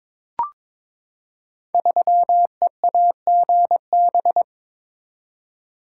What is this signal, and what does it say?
Morse code, heard as a keyed tone, about 22 wpm, 700 Hz. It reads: 3EAGB
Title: Code Groups - 5 Letter:Numbers - 22wpm